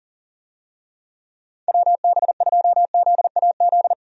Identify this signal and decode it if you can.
Morse code, heard as a keyed tone, about 40 wpm, 700 Hz. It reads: WB2ZAZ
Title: USA Call Signs - 6 Characters - Repeated 3x - 40wpm